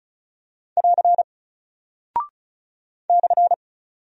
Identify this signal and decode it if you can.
Morse code, heard as a keyed tone, about 36 wpm, 700 Hz. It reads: <AR> /
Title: Single Character (Letter:Number:Punctuation:Prosign) Alt 35wpm